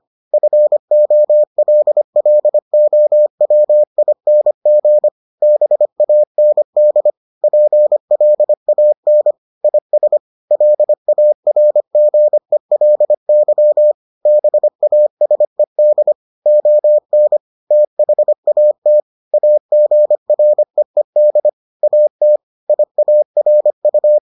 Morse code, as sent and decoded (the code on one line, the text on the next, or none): ..-. --- .-.. .-.. --- .-- .. -. --. / -... .- -. -.. / .--. .-.. .- -. / .. ... / .-.. .- .-. --. .-.. -.-- / -... .- ... -.. / --- -. / - .... .- - / .- --. .-. . -.. / .- - / .. .- .-. ..-
FOLLOWING BAND PLAN IS LARGELY BASED ON THAT AGREED AT IARU